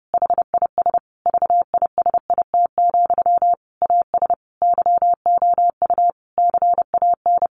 Morse code, heard as a keyed tone, about 30 wpm, 700 Hz. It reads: HIS VISIT, AS YOU CAN
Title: Sentences from Top 1400 Words 30wpm